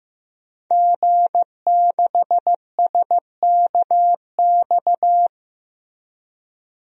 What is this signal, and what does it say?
Morse code, heard as a keyed tone, about 15 wpm, 700 Hz. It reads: G6SKX
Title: DXCC-100 Call Signs - 5 Characters - Repeated 2x - 15wpm